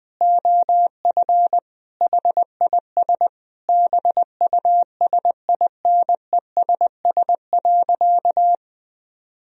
Morse code, heard as a keyed tone, about 20 wpm, 700 Hz. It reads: OF HIS BUSINESS.
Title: Sentences from Top 1000 Words - Review Part 3 - 20wpm